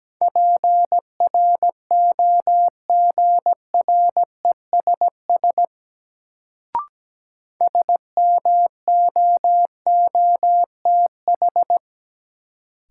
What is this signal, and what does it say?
Morse code, heard as a keyed tone, about 17 wpm, 700 Hz. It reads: PROGRESS SMOOTH
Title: Sentences from Top 1400 Words - New Word Drill - 17wpm